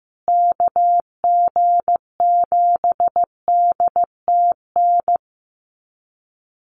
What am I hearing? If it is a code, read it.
Morse code, heard as a keyed tone, about 15 wpm, 700 Hz. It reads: KG7DTN
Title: USA Call Signs Repeated 2x 15wpm